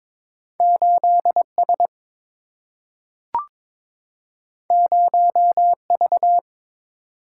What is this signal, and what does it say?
Morse code, heard as a keyed tone, about 22 wpm, 690 Hz. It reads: 8S 0V
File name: practice / 22wpm / Two Letter-Number Combinations - 22wpm